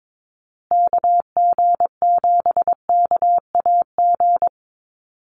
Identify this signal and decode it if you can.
Morse code, heard as a keyed tone, about 22 wpm, 700 Hz. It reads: KG7KAG
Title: USA Call Signs Repeated 3x 22wpm